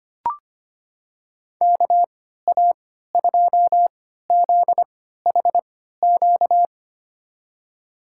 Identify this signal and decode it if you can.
Morse code, heard as a keyed tone, about 25 wpm, spaced as for 14 wpm, 700 Hz. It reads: KA2ZHQ
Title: USA Call Signs - 6 Characters - 3x Character Spacing - 25wpm